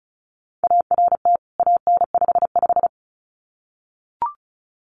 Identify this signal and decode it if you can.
Morse code, heard as a keyed tone, about 34 wpm, 700 Hz. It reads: ART AN55
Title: CWT Exchange 35wpm